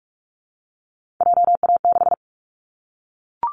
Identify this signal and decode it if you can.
Morse code, heard as a keyed tone, about 44 wpm, 700 Hz. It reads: JA6
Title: Callsign Prefixes for Top 25 DXCC - 45wpm